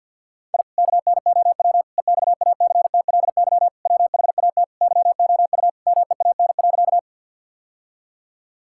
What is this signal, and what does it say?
Morse code, heard as a keyed tone, about 52 wpm, 700 Hz. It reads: I KNOW EXACTLY WHAT YOU MEAN.
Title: Sentences from Top 1000 Words - Review Part 6 - 50wpm